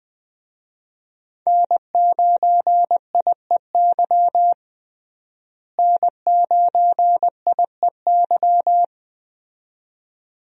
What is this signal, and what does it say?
Morse code, heard as a keyed tone, about 20 wpm, 700 Hz. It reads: N9IEY N9IEY
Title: USA Call Signs - 5 Characters - Repeated 2x - 20wpm